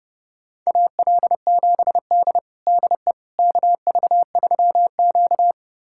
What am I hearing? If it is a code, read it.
Morse code, heard as a keyed tone, about 30 wpm, 700 Hz. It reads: AL7D DE KV3Q
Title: USA Callsign DE Callsign - 4 Characters - 30wpm